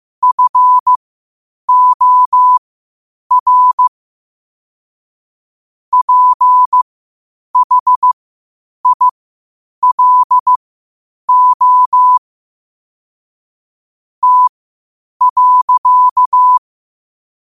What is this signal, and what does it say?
Morse code, heard as a keyed tone, about 15 wpm, spaced as for 8 wpm, 1000 Hz. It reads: FOR PHILO T.
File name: qotdfiles / 8wpm